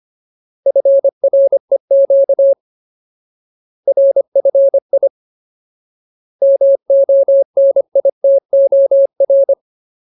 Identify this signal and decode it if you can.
Morse code, heard as a keyed tone, about 25 wpm, 550 Hz. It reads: FREQ RFI MONITOR
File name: Ultimate Word List / 25WPM Ultimate